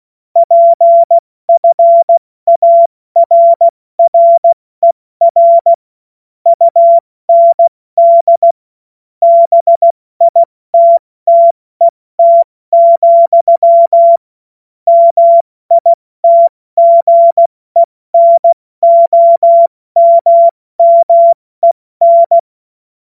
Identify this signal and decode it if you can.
Morse code, heard as a keyed tone, about 16 wpm, 680 Hz. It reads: PFARRER UND BITTET, MITGENOMMEN
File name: Morsekurs / Witze / Witz 20. September 2022 70 F80